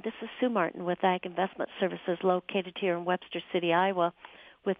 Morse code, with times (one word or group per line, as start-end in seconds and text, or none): none